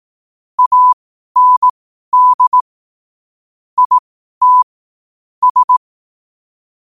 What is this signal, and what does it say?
Morse code, heard as a keyed tone, about 18 wpm, spaced as for 12 wpm, 1000 Hz. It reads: AND IT S
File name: qotdfiles / 12wpm